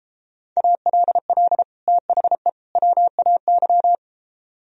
Morse code, heard as a keyed tone, about 29 wpm, 700 Hz. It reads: ALL THE WAY